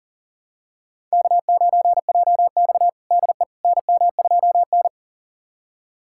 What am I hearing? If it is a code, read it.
Morse code, heard as a keyed tone, about 37 wpm, 700 Hz. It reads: K9JX DE NM2N
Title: USA Callsign DE Callsign - 4 Characters - 40wpm